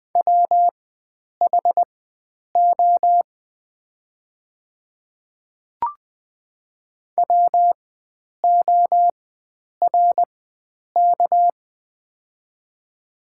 Morse code, heard as a keyed tone, about 20 wpm, spaced as for 9 wpm, 700 Hz. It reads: WHO WORK
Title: 4x Character Spacing - Top 500 Words - 20wpm